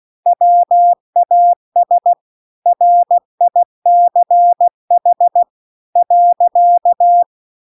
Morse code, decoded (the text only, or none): WAS RICH .